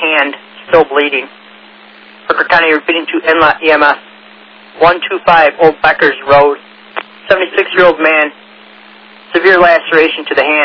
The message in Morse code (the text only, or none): none